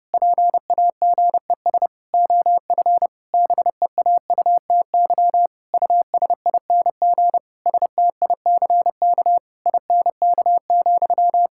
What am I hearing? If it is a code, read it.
Morse code, heard as a keyed tone, about 30 wpm, 700 Hz. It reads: PAGES OF BEAUTY USING STICK INK,
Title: Pangram Challenge v2 - 30wpm